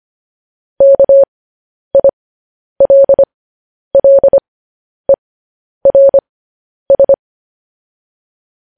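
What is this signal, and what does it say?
Morse code, heard as a keyed tone, about 25 wpm, spaced as for 10 wpm, 550 Hz. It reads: KILLERS
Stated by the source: Morse code news headlines practice at 10 wpm.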